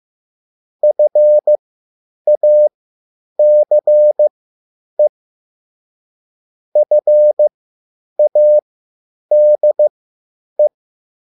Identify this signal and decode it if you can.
Morse code, heard as a keyed tone, about 15 wpm, spaced as for 9 wpm, 600 Hz. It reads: FACE FADE